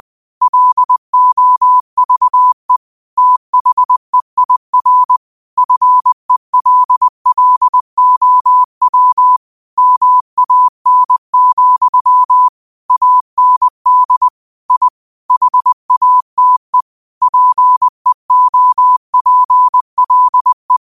Morse code, heard as a keyed tone, about 20 wpm, 1000 Hz. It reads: LOVE THEIR FELLOW MAN, AND I HATE PEOPLE
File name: qotdfiles / 20wpm